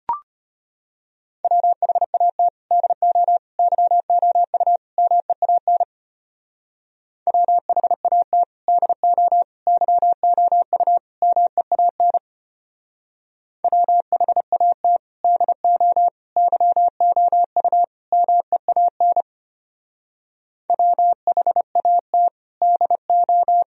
Morse code, heard as a keyed tone, about 31 wpm, 700 Hz. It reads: WHAT DO YOU MEAN WHAT DO YOU MEAN WHAT DO YOU MEAN WHAT DO
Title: Speed-Racing - Sets of 4 Words 25wpm